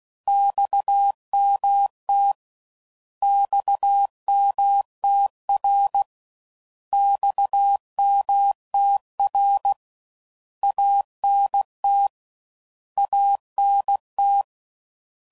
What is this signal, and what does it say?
Morse code, heard as a keyed tone, about 16 wpm, 800 Hz. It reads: XMT XMTR XMTR ANT ANT
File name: Mixed / Abreviations / Abreviations